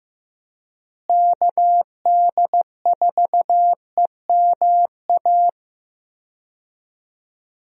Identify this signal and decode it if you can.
Morse code, heard as a keyed tone, about 15 wpm, 700 Hz. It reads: KD4EMA
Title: USA Call Signs - 6 Characters - Repeated 2x - 15wpm